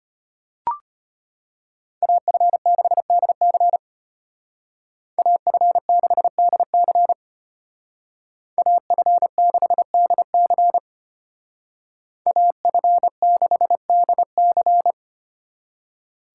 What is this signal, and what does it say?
Morse code, heard as a keyed tone, about 33 wpm, 700 Hz. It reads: AF6DC AF6DC AF6DC AF6DC